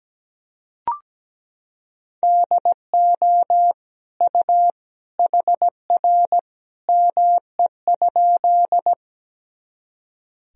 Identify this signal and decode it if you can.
Morse code, heard as a keyed tone, about 17 wpm, 700 Hz. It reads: DO U HR ME?
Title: Sentences from Top 300 Words and Abbreviations 17wpm